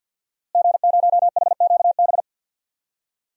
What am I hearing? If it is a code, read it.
Morse code, heard as a keyed tone, about 51 wpm, 700 Hz. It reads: G0HQB
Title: DXCC-100 Call Signs - 5 Characters - Repeated 3x - 50wpm